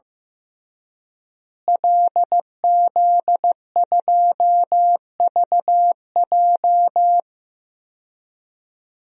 Morse code, heard as a keyed tone, about 15 wpm, 700 Hz. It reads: LZ2VJ
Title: DXCC-100 Call Signs - 5 Characters - Repeated 2x - 15wpm